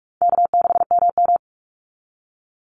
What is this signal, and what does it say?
Morse code, heard as a keyed tone, about 44 wpm, 700 Hz. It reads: K6MM
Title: Call Signs Repeated 2x - 45wpm